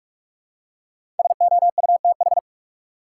Morse code, heard as a keyed tone, about 42 wpm, 700 Hz. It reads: SOUTH
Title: Five Letter Words - 45wpm